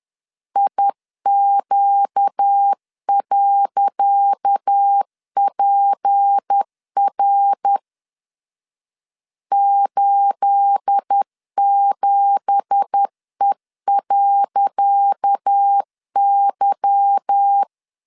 Morse code, heard as a keyed tone, about 10 wpm, 800 Hz. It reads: IQ.PR 87E.Y
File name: LEZIONI / 10 SIGNS / 10C DOT 10wpm 2940